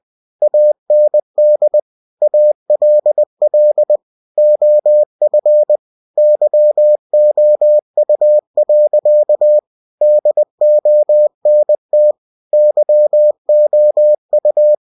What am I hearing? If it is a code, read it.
Morse code, heard as a keyed tone, about 20 wpm, 600 Hz. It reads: AND ALL OF YOU. DONT YOU
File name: cw-books / winnie-the-pooh / winnie-20wpm-page-0011